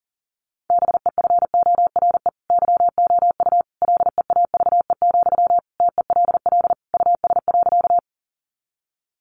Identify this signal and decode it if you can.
Morse code, heard as a keyed tone, about 40 wpm, 700 Hz. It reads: BEFORE YOU LEAVE, TELL US.
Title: Sentences from Top 400 Words 40wpm